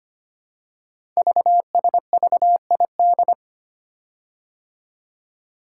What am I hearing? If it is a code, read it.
Morse code, heard as a keyed tone, about 25 wpm, 700 Hz. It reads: VSVID
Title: Code Groups - 5 Letters - 25wpm